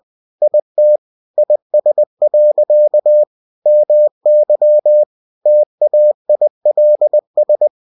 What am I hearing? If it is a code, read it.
Morse code, heard as a keyed tone, about 20 wpm, 600 Hz. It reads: IT IS. MY TAILS
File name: cw-books / winnie-the-pooh / winnie-20wpm-page-0087